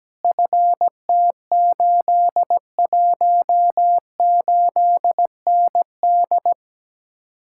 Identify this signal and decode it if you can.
Morse code, heard as a keyed tone, about 17 wpm, 700 Hz. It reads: FT818ND
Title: Speed-Racing Common QSO Phrases Encore - 17wpm